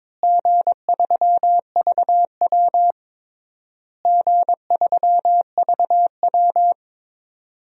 Morse code, heard as a keyed tone, about 22 wpm, 700 Hz. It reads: G3VW G3VW
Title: DXCC-100 Call Signs - 4 Characters - Repeated 3x - 22wpm